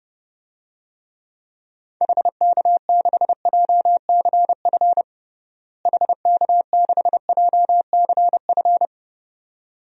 Morse code, heard as a keyed tone, about 29 wpm, 700 Hz. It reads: HK6JCF HK6JCF
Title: DXCC-100 Call Signs - 6 Characters - Repeated 2x - 30wpm